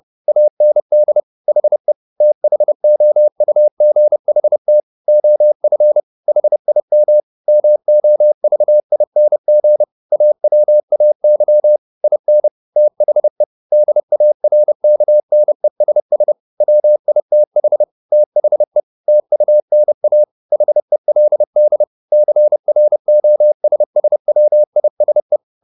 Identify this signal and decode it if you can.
Morse code, heard as a keyed tone, about 30 wpm, 600 Hz. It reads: AND HE THOUGHT OF HIM MOVING AWAY IN THE DARKNESS WITH THE TUNA HELD CROSSWISE
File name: the old man and the sea / the old man and the sea 1 30wpm